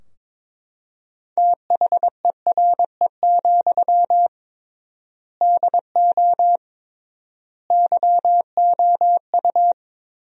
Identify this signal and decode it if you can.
Morse code, heard as a keyed tone, about 22 wpm, 700 Hz. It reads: THERE, DO YOU
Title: 3x Word-Spacing - Sentences from Top 300 Words - 22wpm